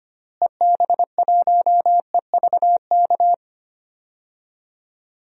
Morse code, heard as a keyed tone, about 25 wpm, 700 Hz. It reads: EB1EVK
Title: DXCC-100 Call Signs - 6 Characters - Repeated 2x - 25wpm